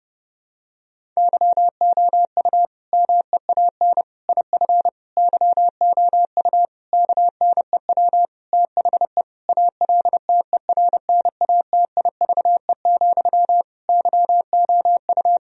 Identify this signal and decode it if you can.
Morse code, heard as a keyed tone, about 32 wpm, 700 Hz. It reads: YOU MEAN IF YOU KNEW THE ALTERNATIVE, YOU
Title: Sentences from Top 1000 Words - Review Part 4 - 30wpm